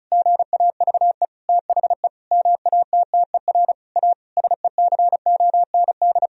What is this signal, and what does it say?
Morse code, heard as a keyed tone, about 36 wpm, 700 Hz. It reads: GAVE THE MATTER A SECOND
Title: Sentences from Top 1000 Words - Review Part 2 - 35wpm